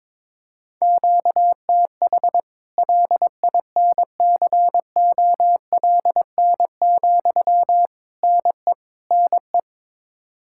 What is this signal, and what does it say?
Morse code, heard as a keyed tone, about 22 wpm, 700 Hz. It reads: QTH LINCOLN, NE NE
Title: Practice QSO - Exchange 1 - Beginning - 22wpm